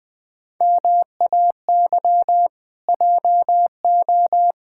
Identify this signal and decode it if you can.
Morse code, heard as a keyed tone, about 20 wpm, 700 Hz. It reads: MAY JO
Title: Pangram Challenge v2 - 20wpm